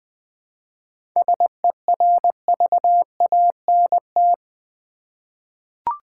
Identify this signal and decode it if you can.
Morse code, heard as a keyed tone, about 20 wpm, 700 Hz. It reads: SERVANT
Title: 7 Letter Words - First 3 Letters Given - 20wpm